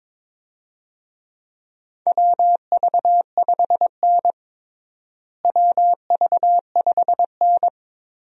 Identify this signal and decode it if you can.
Morse code, heard as a keyed tone, about 22 wpm, 700 Hz. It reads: WV5N WV5N
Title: USA Call Signs - 4 Characters - Repeated 3x - 22wpm